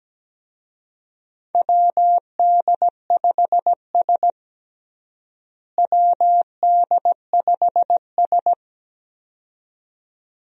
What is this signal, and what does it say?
Morse code, heard as a keyed tone, about 17 wpm, 700 Hz. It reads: WD5S WD5S
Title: USA Call Signs - 4 Characters - Repeated 2x - 17wpm